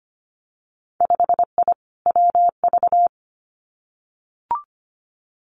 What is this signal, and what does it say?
Morse code, heard as a keyed tone, about 25 wpm, 700 Hz. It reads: 5I WV
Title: Winter Field Day Exchanges Encore 25wpm